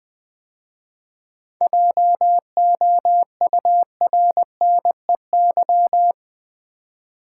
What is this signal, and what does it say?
Morse code, heard as a keyed tone, about 20 wpm, 700 Hz. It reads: JOURNEY
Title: 7 Letter Words - First 3 Letters Given - 20wpm